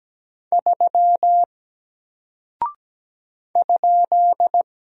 Morse code, heard as a keyed tone, about 17 wpm, 700 Hz. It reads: 3 ?
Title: Single Character (Letter:Number:Punctuation:Prosign) Alt 17wpm